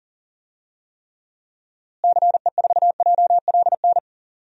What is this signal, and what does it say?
Morse code, heard as a keyed tone, about 40 wpm, 700 Hz. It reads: CE4JLN
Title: DXCC-100 Call Signs - 6 Characters - Repeated 3x - 40wpm